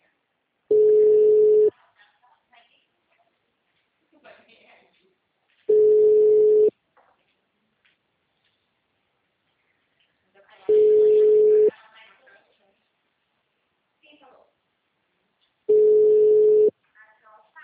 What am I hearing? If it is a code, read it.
Ringback tone (Europe, 425 Hz), with speech.